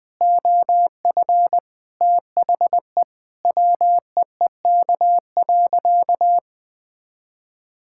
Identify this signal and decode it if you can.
Morse code, heard as a keyed tone, about 21 wpm, 700 Hz. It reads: OF THE WEEK.
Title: Sentences with 4-Word Phrases - 20wpm